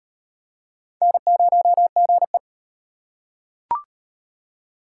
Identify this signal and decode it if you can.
Morse code, heard as a keyed tone, about 38 wpm, 700 Hz. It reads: N0GE